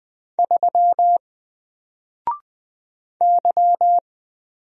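Morse code, heard as a keyed tone, about 20 wpm, 700 Hz. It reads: 3 Y
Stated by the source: Single letter and number Morse drill at 20 wpm, from N4LCD.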